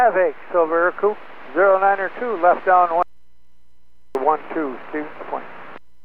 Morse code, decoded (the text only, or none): none